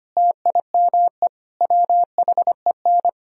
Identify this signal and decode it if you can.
Morse code, heard as a keyed tone, about 32 wpm, 700 Hz. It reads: TIME WHEN